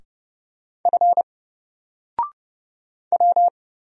Morse code, heard as a keyed tone, about 30 wpm, 700 Hz. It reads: F W
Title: Single Character (Letter:Number:Punctuation:Prosign) 30wpm